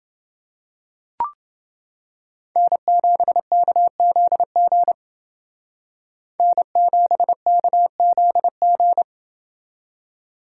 Morse code, heard as a keyed tone, about 29 wpm, 700 Hz. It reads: N7KZG N7KZG